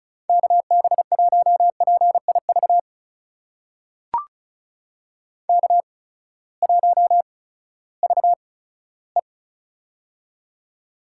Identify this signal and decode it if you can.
Morse code, heard as a keyed tone, about 36 wpm, spaced as for 18 wpm, 700 Hz. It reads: KB1PIV K1VE
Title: USA Call Signs - 8x Character Spacing - 35wpm